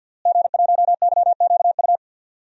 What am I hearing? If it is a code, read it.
Morse code, heard as a keyed tone, about 51 wpm, 700 Hz. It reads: G1YQU
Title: DXCC-100 Call Signs - 5 Characters - Repeated 2x - 50wpm